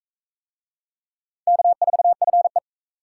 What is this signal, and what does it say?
Morse code, heard as a keyed tone, about 39 wpm, 700 Hz. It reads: K4FE